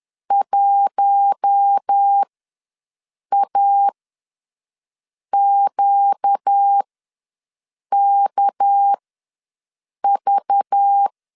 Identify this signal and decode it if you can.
Morse code, heard as a keyed tone, about 10 wpm, spaced as for 5 wpm, 800 Hz. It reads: 1AQKV
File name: LEZIONI / 6 PKX0 / 6 PKX0 ws 10wpm 4220